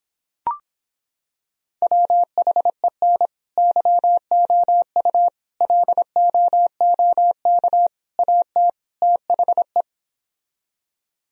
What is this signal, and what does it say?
Morse code, heard as a keyed tone, about 26 wpm, 700 Hz. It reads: WHEN YOU LOOK AT THE